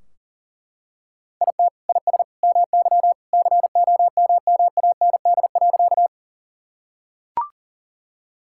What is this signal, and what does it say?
Morse code, heard as a keyed tone, about 40 wpm, 700 Hz. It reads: IT IS MY COMMAND.
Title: Sentences from Top 1600 Words 40wpm